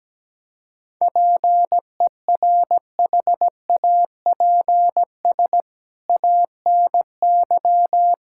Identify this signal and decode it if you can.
Morse code, heard as a keyed tone, about 17 wpm, 700 Hz. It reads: PERHAPS ANY
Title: Sentences from Top 1600 Words 17wpm